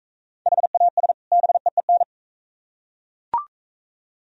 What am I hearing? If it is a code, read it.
Morse code, heard as a keyed tone, about 44 wpm, 690 Hz. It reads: HAS BEEN